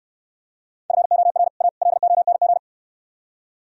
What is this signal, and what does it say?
Morse code, heard as a keyed tone, about 67 wpm, 700 Hz. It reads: FOR A LONG